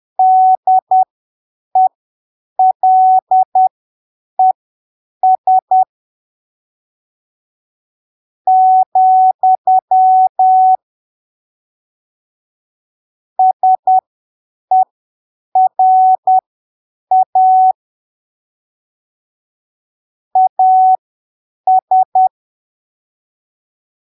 Morse code, heard as a keyed tone, about 10 wpm, spaced as for 7 wpm, 750 Hz. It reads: DELES , SERA AS
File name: arquivos / texto / texto 30 10ppm